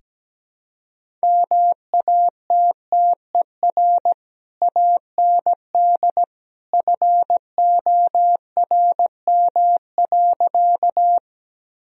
Morse code, heard as a keyed tone, about 17 wpm, 700 Hz. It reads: MATTER AND FORM.
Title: Sentences from Top 1000 Words - Review Part 1 - 17wpm